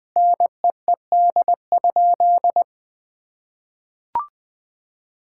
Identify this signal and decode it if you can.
Morse code, heard as a keyed tone, about 20 wpm, 700 Hz. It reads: NEED?
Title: Sentences from Top 1000 Words - Review Part 5 - 20wpm